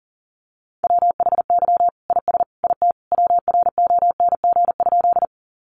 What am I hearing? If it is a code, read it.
Morse code, heard as a keyed tone, about 40 wpm, 700 Hz. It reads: WHY IS IT WRONG?
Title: Sentences from Top 900 Words 40wpm